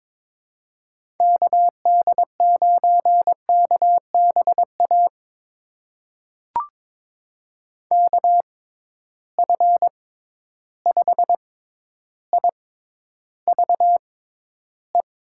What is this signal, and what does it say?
Morse code, heard as a keyed tone, about 21 wpm, spaced as for 11 wpm, 700 Hz. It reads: KD9KBA KF5IVE
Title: USA Call Signs - 6 Characters - 6x Character Spacing - 22wpm